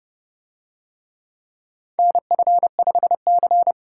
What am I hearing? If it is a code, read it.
Morse code, heard as a keyed tone, about 28 wpm, 700 Hz. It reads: NF5C